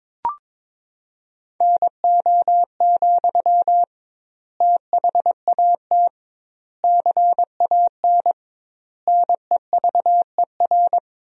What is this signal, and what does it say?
Morse code, heard as a keyed tone, about 22 wpm, 700 Hz. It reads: NO, THAT CAN NEVER
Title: 2x Word-Spacing - Sentences from Top 300 Words - 22wpm